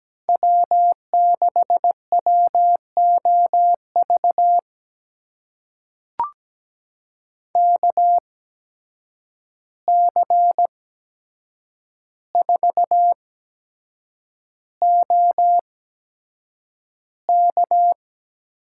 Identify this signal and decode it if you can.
Morse code, heard as a keyed tone, about 17 wpm, spaced as for 7 wpm, 700 Hz. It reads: W6WOV KC4OK
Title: USA Call Signs - 5 Characters - 8x Character Spacing - 17wpm